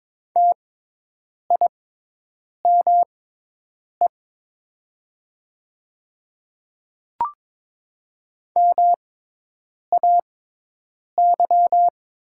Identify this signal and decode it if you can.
Morse code, heard as a keyed tone, about 22 wpm, spaced as for 8 wpm, 700 Hz. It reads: TIME MAY
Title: 6x Character Spacing - Top 200 Words - 22wpm